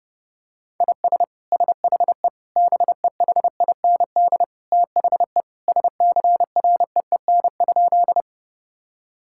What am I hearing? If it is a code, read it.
Morse code, heard as a keyed tone, about 30 wpm, 700 Hz. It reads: IS SHE BEHIND THE SCREEN?
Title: Sentences from Top 1000 Words - Review Part 6 - 30wpm